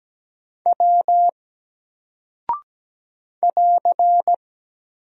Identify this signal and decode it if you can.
Morse code, heard as a keyed tone, about 17 wpm, 700 Hz. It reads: W <AR>
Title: Single Character (Letter:Number:Punctuation:Prosign) 17wpm